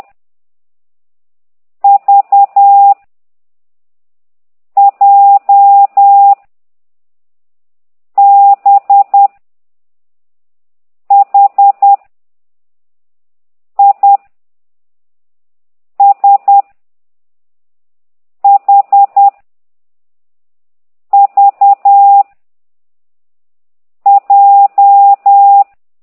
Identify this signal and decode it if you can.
Morse code, heard as a keyed tone, about 10 wpm, 800 Hz. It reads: V J B H I S H V J